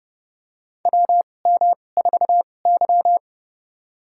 Morse code, heard as a keyed tone, about 30 wpm, spaced as for 22 wpm, 700 Hz. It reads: WM4Y
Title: USA Call Signs - 4 Characters - 2x Character Spacing - 30wpm